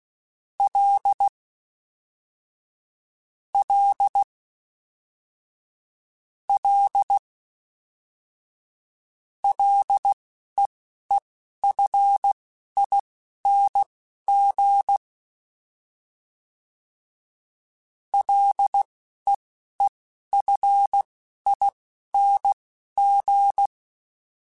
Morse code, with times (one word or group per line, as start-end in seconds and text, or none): 0.6-1.3 L
3.5-4.2 L
6.5-7.2 L
9.4-15.0 LEEFING
18.1-23.7 LEEFING